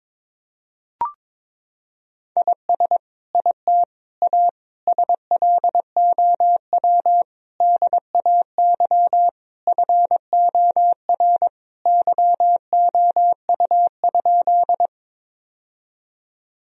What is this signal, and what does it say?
Morse code, heard as a keyed tone, about 22 wpm, 700 Hz. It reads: IS IT A SLOW DAY FOR YOU?
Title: Sentences from Top 1000 Words - Review Part 2 - 22wpm